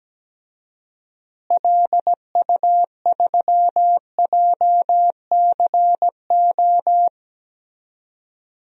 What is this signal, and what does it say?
Morse code, heard as a keyed tone, about 17 wpm, 700 Hz. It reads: LU3JCO
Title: DXCC-100 Call Signs - 6 Characters - Repeated 3x - 17wpm